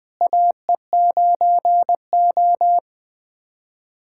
Morse code, heard as a keyed tone, about 20 wpm, 700 Hz. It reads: AE9O
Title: USA Callsign DE Callsign - 4 Characters - 20wpm